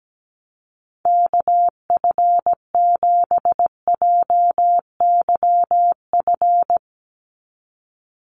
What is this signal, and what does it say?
Morse code, heard as a keyed tone, about 17 wpm, 700 Hz. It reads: KF7JYF
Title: USA Call Signs 17wpm